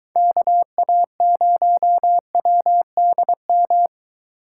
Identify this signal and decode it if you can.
Morse code, heard as a keyed tone, about 23 wpm, 700 Hz. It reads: KA0WDM